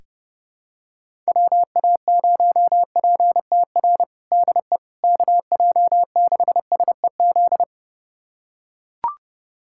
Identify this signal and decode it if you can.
Morse code, heard as a keyed tone, about 32 wpm, 700 Hz. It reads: WA0PTR DE KJ6SEZ
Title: USA Callsign DE Callsign - 6 Characters - 30wpm